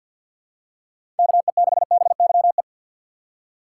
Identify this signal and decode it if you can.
Morse code, heard as a keyed tone, about 50 wpm, 700 Hz. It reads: KE6BYE
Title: USA Call Signs - 6 Characters - 3x Character Spacing - 50wpm